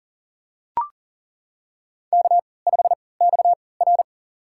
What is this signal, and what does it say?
Morse code, heard as a keyed tone, about 37 wpm, spaced as for 22 wpm, 700 Hz. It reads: K5XR
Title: USA Call Signs - 4 Characters - 3x Character Spacing - 40wpm